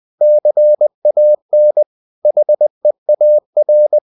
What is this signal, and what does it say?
Morse code, heard as a keyed tone, about 20 wpm, 600 Hz. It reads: CAN HEAR
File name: cw-books / winnie-the-pooh / winnie-20wpm-page-0054